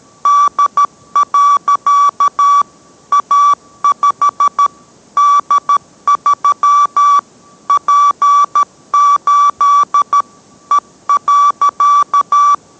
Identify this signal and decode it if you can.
Morse code, heard as a keyed tone, about 13 wpm, 1150 Hz. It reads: D. A5 D3 P8 E.